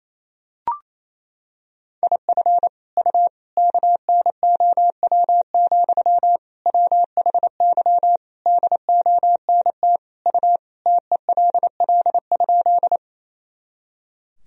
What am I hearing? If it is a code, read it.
Morse code, heard as a keyed tone, about 28 wpm, 700 Hz. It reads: IF U KNOW, WHY DONT U TELL?
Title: Sentences from Top 300 Words and Abbreviations 28wpm